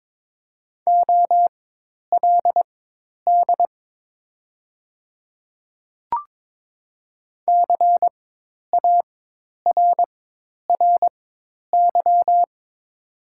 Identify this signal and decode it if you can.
Morse code, heard as a keyed tone, about 22 wpm, spaced as for 10 wpm, 700 Hz. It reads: OLD CARRY
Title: 4x Character Spacing - Top 300 Words - 22wpm